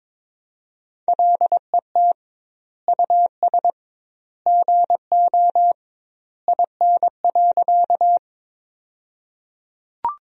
Morse code, heard as a keyed tone, about 22 wpm, 700 Hz. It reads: LET US GO IN.
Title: 2x Word-Spacing - Sentences from Top 300 Words - 22wpm